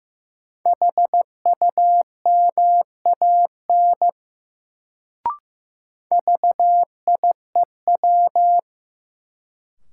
Morse code, heard as a keyed tone, about 15 wpm, 700 Hz. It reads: HUMAN VIEW
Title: Sets of 5 Words - New Word Drill - 15wpm